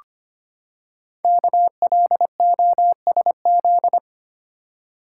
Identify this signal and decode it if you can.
Morse code, heard as a keyed tone, about 25 wpm, 700 Hz. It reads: KLOSZ
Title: Code Groups - 5 Letters - 25wpm